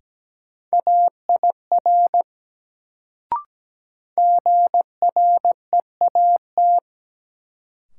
Morse code, heard as a keyed tone, about 17 wpm, 700 Hz. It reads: AIR GREAT
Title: Sentences from Top 200 Words - New Word Drill - 17wpm